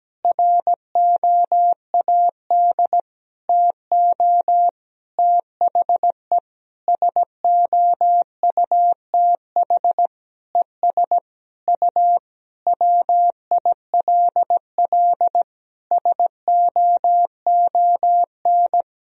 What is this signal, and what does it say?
Morse code, heard as a keyed tone, about 17 wpm, 700 Hz. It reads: ROAD TO THE SOUTH ES U WILL SOON
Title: Sentences from Top 400 Words and Abbreviations 17wpm